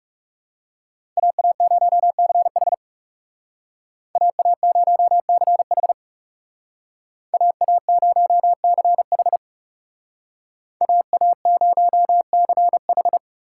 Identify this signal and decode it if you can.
Morse code, heard as a keyed tone, about 37 wpm, 700 Hz. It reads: AA0CH AA0CH AA0CH AA0CH
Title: Speed Racing - USA 2x2 Call Signs 30wpm